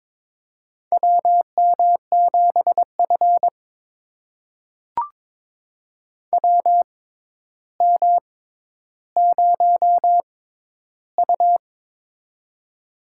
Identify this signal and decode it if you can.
Morse code, heard as a keyed tone, about 22 wpm, spaced as for 11 wpm, 700 Hz. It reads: WM7F WM0U
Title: USA Call Signs - 4 Characters - 6x Character Spacing - 22wpm